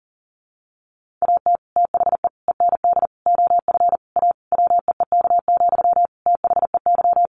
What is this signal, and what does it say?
Morse code, heard as a keyed tone, about 37 wpm, 700 Hz. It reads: AT THE END OF A WEEK, THEY
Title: Sentences from Top 1400 Words 40wpm